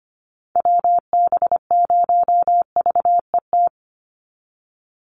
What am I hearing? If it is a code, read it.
Morse code, heard as a keyed tone, about 25 wpm, 700 Hz. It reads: WB0VET
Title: USA Call Signs 25wpm